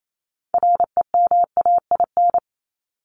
Morse code, heard as a keyed tone, about 28 wpm, 700 Hz. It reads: REMAIN
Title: ES:S Suffix - Top 100 Words with Base Words 28wpm